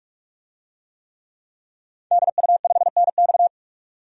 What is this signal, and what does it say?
Morse code, heard as a keyed tone, about 46 wpm, 700 Hz. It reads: DU5NX
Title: DXCC-100 Call Signs - 5 Characters - 45wpm